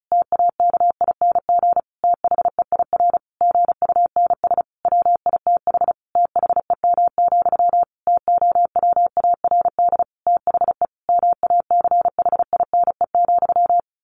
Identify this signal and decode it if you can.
Morse code, heard as a keyed tone, about 34 wpm, 700 Hz. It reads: TAKING THEIR GUNS WITH THEM, TOWARD THE MACHINE,
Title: A Sound Of Thunder 35wpm